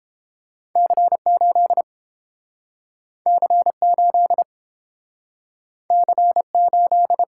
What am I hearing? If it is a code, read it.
Morse code, heard as a keyed tone, about 30 wpm, 700 Hz. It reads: C8 C8 C8